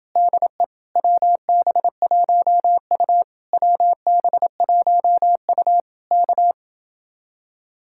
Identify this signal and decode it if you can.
Morse code, heard as a keyed tone, about 30 wpm, 700 Hz. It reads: DE WB1U WB1U K